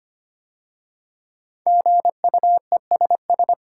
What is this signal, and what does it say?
Morse code, heard as a keyed tone, about 25 wpm, 700 Hz. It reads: GUESS
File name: 5 Letter Words - First 3 Letters Given - 25wpm